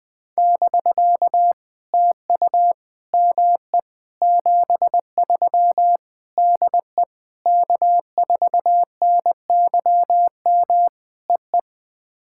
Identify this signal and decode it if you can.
Morse code, heard as a keyed tone, about 20 wpm, 700 Hz. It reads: <BK> TU ME 73 DE K4NYM EE
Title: POTA QSO Practice - 20wpm